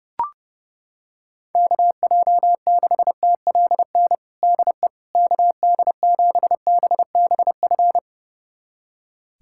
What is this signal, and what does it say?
Morse code, heard as a keyed tone, about 31 wpm, 700 Hz. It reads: KJ6TLN DE KD7BBF
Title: USA Callsign DE Callsign - 6 Characters - 30wpm